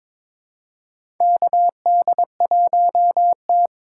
Morse code, heard as a keyed tone, about 22 wpm, 700 Hz. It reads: KD1T
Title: USA Call Signs - 4 Characters - 6x Character Spacing - 22wpm